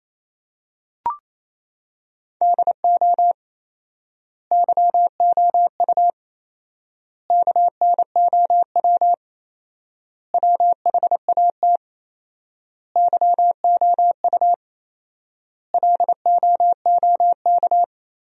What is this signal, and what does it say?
Morse code, heard as a keyed tone, about 28 wpm, 700 Hz. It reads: DO YOU KNOW WHAT YOU LOOK
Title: 4x Word-Spacing - Sentences from Top 100 Words - 28wpm